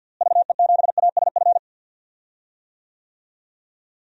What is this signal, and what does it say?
Morse code, heard as a keyed tone, about 52 wpm, 700 Hz. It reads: VE7ASF
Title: DXCC-100 Call Signs - 6 Characters - Repeated 3x - 50wpm